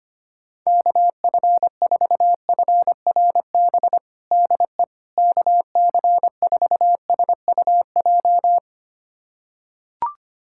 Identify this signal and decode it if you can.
Morse code, heard as a keyed tone, about 25 wpm, 700 Hz. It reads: KF4FRB DE KC4SUJ
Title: USA Callsign DE Callsign - 6 Characters - 25wpm